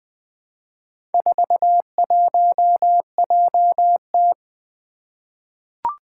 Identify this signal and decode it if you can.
Morse code, heard as a keyed tone, about 20 wpm, 700 Hz. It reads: 41JT
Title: Four Challenging Letter-Number Combinations 20wpm